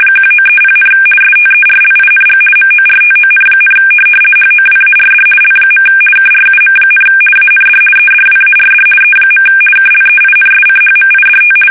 RTTY, about 50 baud. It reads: 1930, STERLING MORTON, HOWARD L. KRUM, AND EDWARD E. KLEINSCHMIDT
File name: fsk850-50B / teletype-corp